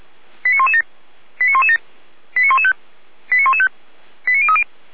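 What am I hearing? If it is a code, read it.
Five-tone selective call (ZVEI), heard as quick runs of five tones, tones of about 70 ms. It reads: 89117 89117 89115 89115 89030
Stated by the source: ZVEI five-tone selective call received on an SDR, logged by the Feuerwehr Orlamünde fire brigade server.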